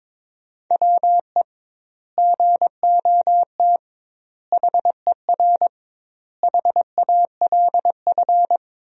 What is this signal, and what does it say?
Morse code, heard as a keyed tone, about 22 wpm, 700 Hz. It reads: WE GOT HER HALF
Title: 2x Word-Spacing - Sentences from Top 400 Words - 22wpm